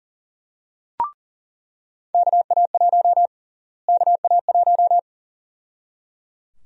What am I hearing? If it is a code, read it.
Morse code, heard as a keyed tone, about 37 wpm, 700 Hz. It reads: KA1 KA1
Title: USA Call Sign Prefixes - Repeated 2x - 40wpm